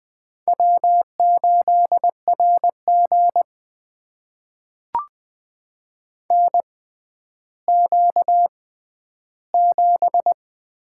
Morse code, heard as a keyed tone, about 20 wpm, spaced as for 11 wpm, 700 Hz. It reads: W8RG NQ7
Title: USA Call Signs - 6x Character Spacing - 20wpm